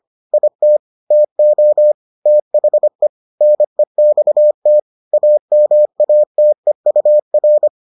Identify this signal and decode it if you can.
Morse code, heard as a keyed tone, about 25 wpm, 600 Hz. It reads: IT TO THE NEXT AMATEUR